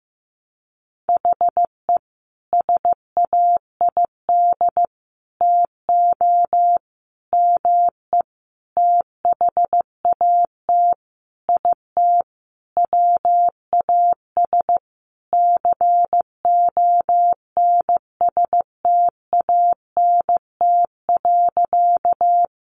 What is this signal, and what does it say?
Morse code, heard as a keyed tone, about 15 wpm, 700 Hz. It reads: HE SAID TO ME THAT IT WAS CONSTANT.
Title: Sentences from Top 1500 Words 15wpm